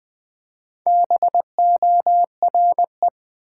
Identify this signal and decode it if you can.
Morse code, heard as a keyed tone, about 20 wpm, 700 Hz. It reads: BORE
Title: Build A Word - 1 Letter At A Time - 3 to 5-Letter Words - 20wpm